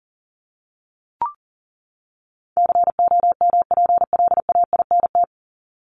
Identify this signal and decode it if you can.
Morse code, heard as a keyed tone, about 40 wpm, 700 Hz. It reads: COMPLAINT
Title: Top 1251-1500 Words 40wpm